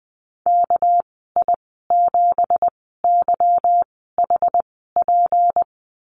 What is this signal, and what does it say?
Morse code, heard as a keyed tone, about 20 wpm, spaced as for 15 wpm, 700 Hz. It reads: KI7YHP
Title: 231 Lesson 37g - Letter x - Call Signs 20wpm